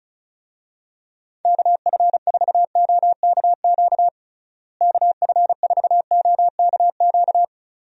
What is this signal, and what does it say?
Morse code, heard as a keyed tone, about 34 wpm, 700 Hz. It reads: KF4OKQ KF4OKQ
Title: USA Call Signs - 6 Characters - Repeated 3x - 35wpm